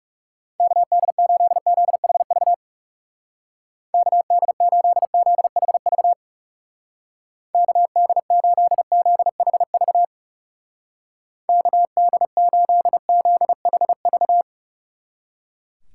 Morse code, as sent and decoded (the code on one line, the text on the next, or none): -.- -.. ---.. --.. .... ...- / -.- -.. ---.. --.. .... ...- / -.- -.. ---.. --.. .... ...- / -.- -.. ---.. --.. .... ...-
KD8ZHV KD8ZHV KD8ZHV KD8ZHV